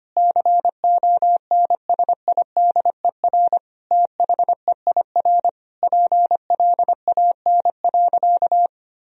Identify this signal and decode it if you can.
Morse code, heard as a keyed tone, about 25 wpm, 690 Hz. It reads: CONSIDER THEIR PLAN.
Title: Sentences from Top 1000 Words - Review Part 4 - 25wpm